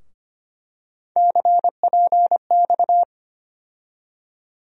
Morse code, heard as a keyed tone, about 25 wpm, 700 Hz. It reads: CPX
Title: USA Call Sign Postfixes - 3 Letters 25wpm